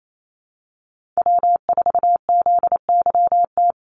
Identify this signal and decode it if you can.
Morse code, heard as a keyed tone, about 28 wpm, 700 Hz. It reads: W4ZYT
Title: Call Signs - 28wpm